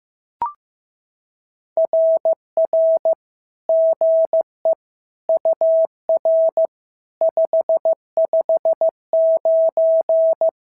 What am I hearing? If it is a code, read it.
Morse code, heard as a keyed tone, about 14 wpm, 650 Hz. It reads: RR GE UR 559
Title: SOTA QSO Practice 15wpm